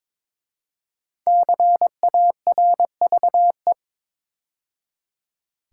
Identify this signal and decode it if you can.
Morse code, heard as a keyed tone, about 22 wpm, 700 Hz. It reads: CARVE
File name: 5 Letter Words - First 3 Letters Given - 22wpm